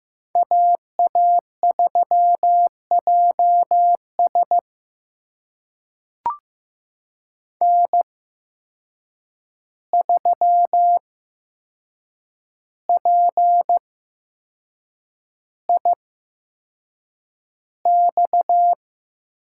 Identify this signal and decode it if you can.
Morse code, heard as a keyed tone, about 15 wpm, spaced as for 6 wpm, 700 Hz. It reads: AA3JS N3PIX
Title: USA Call Signs - 5 Characters - 8x Character Spacing - 15wpm